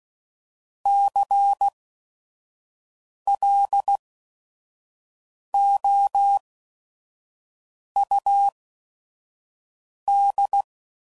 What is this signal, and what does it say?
Morse code, heard as a keyed tone, about 16 wpm, spaced as for 5 wpm, 800 Hz. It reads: CLOUD
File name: Mixed / Misc / WordPairs